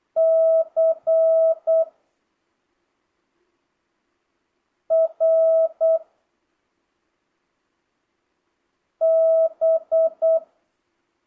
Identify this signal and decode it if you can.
Morse code, heard as a keyed tone, about 8 wpm, 650 Hz. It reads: C R B
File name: jh8cbh / 2025cw